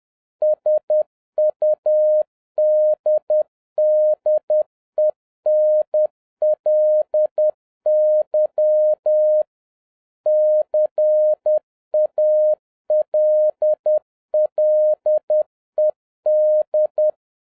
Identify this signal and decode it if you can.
Morse code, heard as a keyed tone, about 10 wpm, 610 Hz. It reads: SUDDENLY CALLED